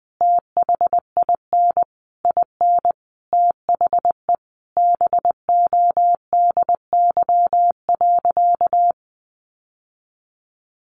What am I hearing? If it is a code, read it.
Morse code, heard as a keyed tone, about 20 wpm, 700 Hz. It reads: THIN IN THE BODY.
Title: Sentences from Top 1200 Words 20wpm